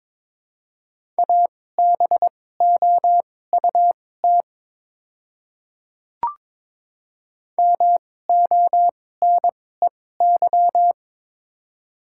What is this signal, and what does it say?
Morse code, heard as a keyed tone, about 21 wpm, spaced as for 16 wpm, 700 Hz. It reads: ABOUT MONEY
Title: 2x Character Spacing - Top 500 Words - 22wpm